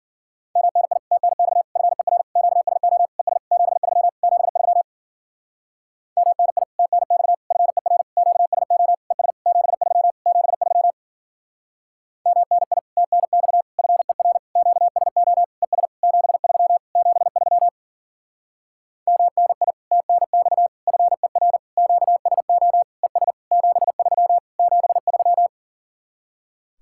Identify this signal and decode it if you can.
Morse code, heard as a keyed tone, about 50 wpm, 700 Hz. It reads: MNI TNX FER QSO ES 73 73 MNI TNX FER QSO ES 73 73 MNI TNX FER QSO ES 73 73 MNI TNX FER QSO ES 73 73
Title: Speed-Racing Common QSO Phrases - 40wpm